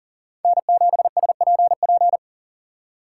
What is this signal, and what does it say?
Morse code, heard as a keyed tone, about 40 wpm, 700 Hz. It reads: N7SPP
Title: USA Call Signs - 2x Character Spacing - 40wpm